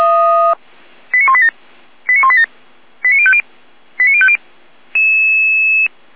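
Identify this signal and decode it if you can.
Five-tone selective call (ZVEI), heard as quick runs of five tones, tones of about 70 ms. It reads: 89187 89187 89050 89050
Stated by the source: ZVEI five-tone call 89191 received on SDR.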